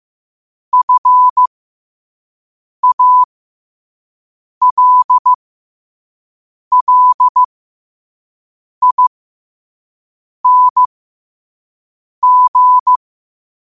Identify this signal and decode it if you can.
Morse code, heard as a keyed tone, about 15 wpm, spaced as for 5 wpm, 1000 Hz. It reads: FALLING
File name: qotdfiles / 5wpm